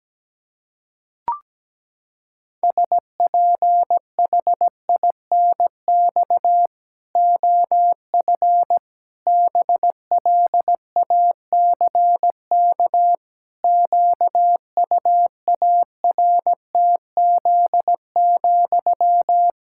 Morse code, heard as a keyed tone, about 17 wpm, 700 Hz. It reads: SPHINX OF BLACK QUARTZ,
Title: Pangram Challenge v2 - 17wpm